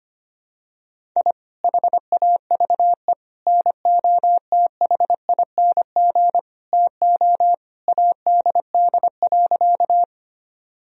Morse code, heard as a keyed tone, about 25 wpm, 700 Hz. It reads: I HAVE NOTHING TO ADD.
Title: Sentences from Top 1000 Words - Review Part 1 - 25wpm